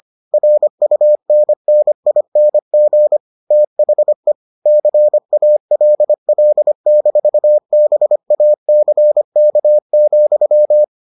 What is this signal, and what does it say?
Morse code, heard as a keyed tone, about 25 wpm, 600 Hz. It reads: RUNNING THE CALL-BACK,